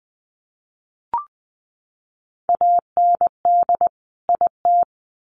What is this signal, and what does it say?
Morse code, heard as a keyed tone, about 20 wpm, 700 Hz. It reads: AND IT
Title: Sets of 2 Words 20wpm